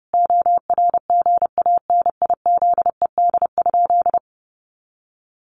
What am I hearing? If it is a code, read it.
Morse code, heard as a keyed tone, about 30 wpm, 700 Hz. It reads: ORGANIZED?
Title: Sentences from Top 1300 Words 30wpm